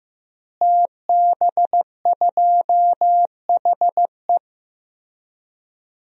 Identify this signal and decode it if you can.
Morse code, heard as a keyed tone, about 15 wpm, 700 Hz. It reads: TB2HE
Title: DXCC-100 Call Signs - 5 Characters - Repeated 3x - 15wpm